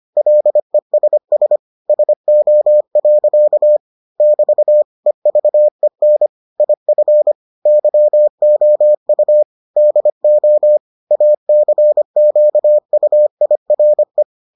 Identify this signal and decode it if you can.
Morse code, heard as a keyed tone, about 25 wpm, 600 Hz. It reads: LESS SO. <BT> EVEN IF YOU DO ACQUIRE